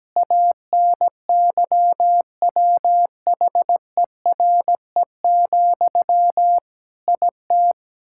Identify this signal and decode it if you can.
Morse code, heard as a keyed tone, about 17 wpm, 700 Hz. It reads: ANYWHERE, IT
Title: Sentences from Top 1600 Words 17wpm